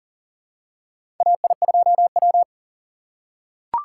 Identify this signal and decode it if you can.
Morse code, heard as a keyed tone, about 40 wpm, 700 Hz. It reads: AI2W
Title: USA Call Signs - 6x Character Spacing - 40wpm